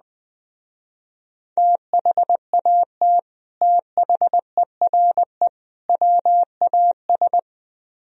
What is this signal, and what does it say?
Morse code, heard as a keyed tone, about 20 wpm, 700 Hz. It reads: THAT THERE WAS